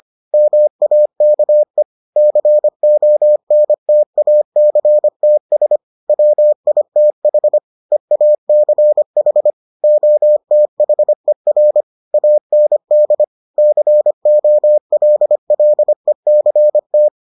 Morse code, heard as a keyed tone, about 25 wpm, 600 Hz. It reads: MAKE CONTACTS WITH EACH OTHER AND COLLECT